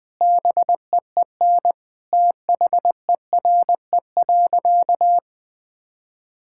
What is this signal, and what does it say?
Morse code, heard as a keyed tone, about 20 wpm, 700 Hz. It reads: BEEN THERE.
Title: Sentences with 4-Word Phrases - 20wpm